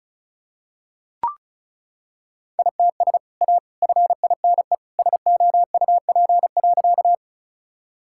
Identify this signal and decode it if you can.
Morse code, heard as a keyed tone, about 36 wpm, 690 Hz. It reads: ITS A FINE SOUP.
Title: Sentences from Top 1000 Words - Review Part 6 - 35wpm